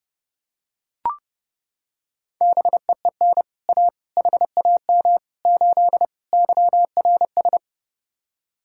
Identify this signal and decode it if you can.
Morse code, heard as a keyed tone, about 30 wpm, 700 Hz. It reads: BEEN A HAM 8 YRS